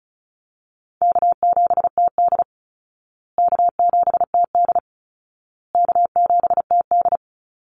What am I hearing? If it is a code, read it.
Morse code, heard as a keyed tone, about 34 wpm, 700 Hz. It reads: K7TD K7TD K7TD
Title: Easy Call Signs Repeated 3x - 35wpm